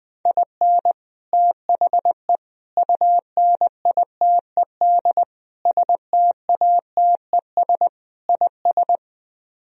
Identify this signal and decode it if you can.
Morse code, heard as a keyed tone, about 21 wpm, 700 Hz. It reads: IN THE UNITED STATES IS
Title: Speed-Racing - Sets of 5 Words 20wpm